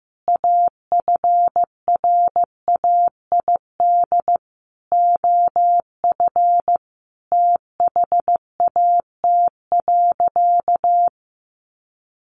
Morse code, heard as a keyed tone, about 15 wpm, 700 Hz. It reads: AFRAID OF THAT.
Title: Sentences from Top 1500 Words 15wpm